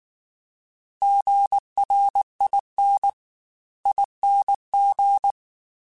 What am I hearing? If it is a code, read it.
Morse code, heard as a keyed tone, about 19 wpm, 800 Hz. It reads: GRIN ING